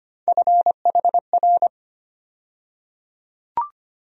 Morse code, heard as a keyed tone, about 25 wpm, 700 Hz. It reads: FHR
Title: USA Call Sign Postfixes - 3 Letters 25wpm